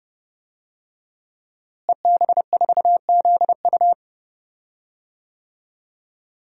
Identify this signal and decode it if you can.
Morse code, heard as a keyed tone, about 30 wpm, 700 Hz. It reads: EB4ZU
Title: DXCC-100 Call Signs - 5 Characters - Repeated 2x - 30wpm